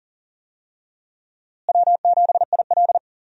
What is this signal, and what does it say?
Morse code, heard as a keyed tone, about 40 wpm, 700 Hz. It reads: W7IL
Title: USA Call Signs - 4 Characters - Repeated 2x - 40wpm